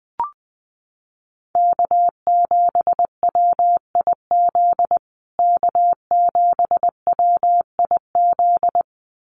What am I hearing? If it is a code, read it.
Morse code, heard as a keyed tone, about 19 wpm, 700 Hz. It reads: K7WIZ K7WIZ
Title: USA Call Signs Repeated 2x 20wpm